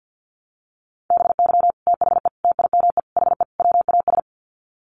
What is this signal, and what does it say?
Morse code, heard as a keyed tone, about 50 wpm, 700 Hz. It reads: BY THE TIME HE WAS
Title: Sets of 5 Words Encore 50wpm